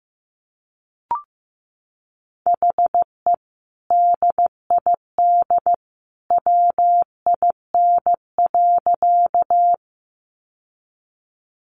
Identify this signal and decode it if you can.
Morse code, heard as a keyed tone, about 15 wpm, 700 Hz. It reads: HE DID WIN.
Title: Sentences from Top 1500 Words 15wpm